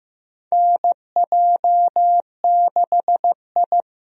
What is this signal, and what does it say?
Morse code, heard as a keyed tone, about 14 wpm, 700 Hz. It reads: NJ6I
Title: USA Call Signs - 4 Characters - Repeated 2x - 15wpm